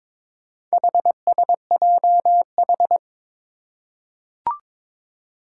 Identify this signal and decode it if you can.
Morse code, heard as a keyed tone, about 22 wpm, 700 Hz. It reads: HSJH
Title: Four Challenging Letter-Number Combinations 22wpm